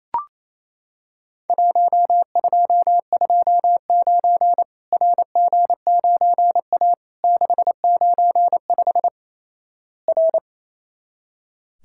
Morse code, heard as a keyed tone, about 28 wpm, 700 Hz. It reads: 1229 RG9A 695 R
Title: Worked All Europe DX Contest - QTC Exchange - 28wpm